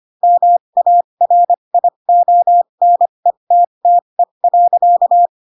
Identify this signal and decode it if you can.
Morse code, heard as a keyed tone, about 25 wpm, spaced as for 22 wpm, 700 Hz. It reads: MARIONETTE.
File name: Pinocchio / Pinocchio ch3 18 25 ik1hge